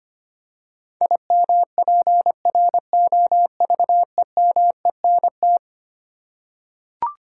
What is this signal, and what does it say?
Morse code, heard as a keyed tone, about 25 wpm, 700 Hz. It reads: IMPROVEMENT
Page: morse-code-ninja-practice-25wpm